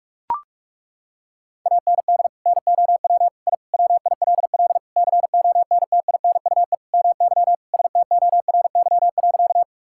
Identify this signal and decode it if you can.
Morse code, heard as a keyed tone, about 41 wpm, 700 Hz. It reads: AND NOW I WILL CONTINUE MY STORY.
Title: Sentences from Top 1000 Words - Review Part 6 - 45wpm